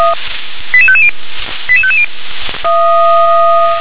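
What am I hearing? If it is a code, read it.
Five-tone selective call (ZVEI), heard as quick runs of five tones, tones of about 70 ms. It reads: 88550 88550
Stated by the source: ZVEI five-tone selective call received on SDR.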